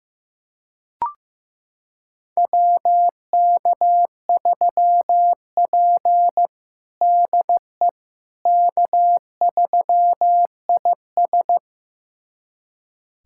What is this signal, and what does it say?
Morse code, heard as a keyed tone, about 15 wpm, 700 Hz. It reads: WK3P DE K3IS
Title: USA Callsign DE Callsign - 4 Characters - 15wpm